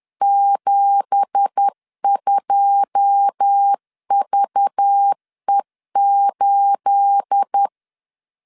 Morse code, heard as a keyed tone, about 10 wpm, 800 Hz. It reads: 72VE8
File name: LEZIONI / 9 2867 10wpm 2733